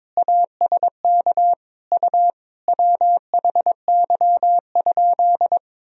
Morse code, heard as a keyed tone, about 22 wpm, 700 Hz. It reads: ASK U WHY?
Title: Sentences from Top 200 Words and Abbreviations 22wpm